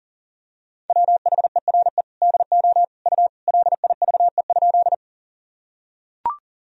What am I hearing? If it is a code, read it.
Morse code, heard as a keyed tone, about 40 wpm, 700 Hz. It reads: WHERE DO U LIVE?
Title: Sentences from Top 200 Words and Abbreviations 40wpm